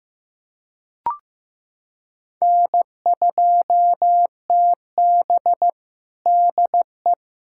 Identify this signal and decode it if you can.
Morse code, heard as a keyed tone, about 15 wpm, 700 Hz. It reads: N2TB DE
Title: USA Callsign DE Callsign - 4 Characters - 15wpm